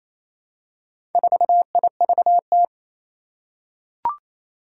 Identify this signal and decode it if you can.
Morse code, heard as a keyed tone, about 27 wpm, 700 Hz. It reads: 4IVT
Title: Four Challenging Letter-Number Combinations 28wpm